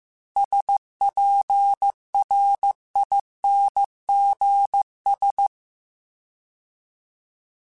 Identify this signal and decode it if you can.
Morse code, heard as a keyed tone, about 15 wpm, 800 Hz. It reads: SPRINGS
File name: fromSMs / FromSMs mp3s / Cities@13